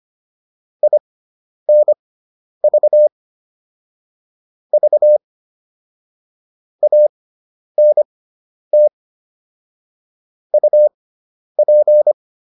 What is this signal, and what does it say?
Morse code, heard as a keyed tone, about 25 wpm, spaced as for 10 wpm, 600 Hz. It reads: INV V ANT UP
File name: QSO302 10